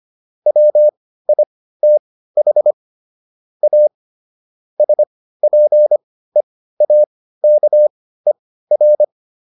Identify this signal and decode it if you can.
Morse code, heard as a keyed tone, about 25 wpm, spaced as for 15 wpm, 600 Hz. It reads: WITH A SPEAKER